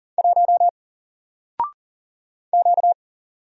Morse code, heard as a keyed tone, about 38 wpm, 700 Hz. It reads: 1 Q
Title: Single Letter-Number 40wpm